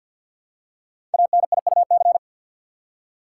Morse code, heard as a keyed tone, about 45 wpm, 700 Hz. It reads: ANIUC
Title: Code Groups - 5 Letter:Numbers - 50wpm